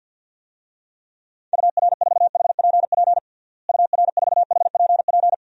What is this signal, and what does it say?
Morse code, heard as a keyed tone, about 51 wpm, 700 Hz. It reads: UR4HPP UR4HPP
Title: DXCC-100 Call Signs - 6 Characters - Repeated 3x - 50wpm